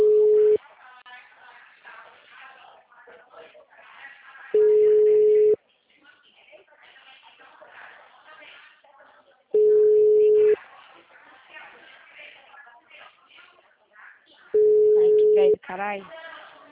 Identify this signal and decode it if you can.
Ringback tone (Europe, 425 Hz), with speech.